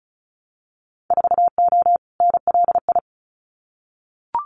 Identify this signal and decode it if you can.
Morse code, heard as a keyed tone, about 34 wpm, 700 Hz. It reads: 4O NLI
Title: Winter Field Day Exchanges Encore 35wpm